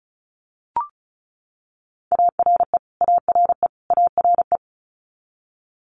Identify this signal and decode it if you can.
Morse code, heard as a keyed tone, about 36 wpm, 700 Hz. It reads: ARE ARE ARE
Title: Multi-letter Phonemes Repeated 3x 35wpm